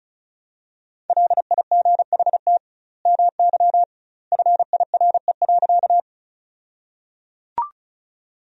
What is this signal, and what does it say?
Morse code, heard as a keyed tone, about 34 wpm, 700 Hz. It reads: LIGHT MY FIRE.
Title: 2x Word-Spacing - Sentences from Top 400 Words - 35wpm